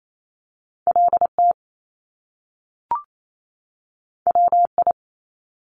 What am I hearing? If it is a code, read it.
Morse code, heard as a keyed tone, about 28 wpm, 700 Hz. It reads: LT WI
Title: Top Two Letter Combinations in Words 28wpm